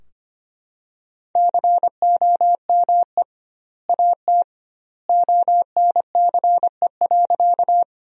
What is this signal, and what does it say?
Morse code, heard as a keyed tone, about 25 wpm, 700 Hz. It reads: COME AT ONCE.
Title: 2x Word-Spacing - Sentences from Top 300 Words - 25wpm